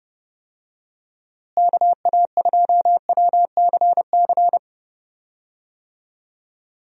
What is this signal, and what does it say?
Morse code, heard as a keyed tone, about 30 wpm, 700 Hz. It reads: KA2WCC
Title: USA Call Signs - 6 Characters - Repeated 3x - 30wpm